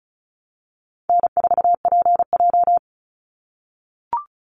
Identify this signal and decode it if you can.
Morse code, heard as a keyed tone, about 34 wpm, 700 Hz. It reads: N4PJ
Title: Call Signs - 35wpm